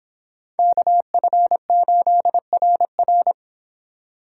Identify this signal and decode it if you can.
Morse code, heard as a keyed tone, about 25 wpm, 700 Hz. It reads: KF8RR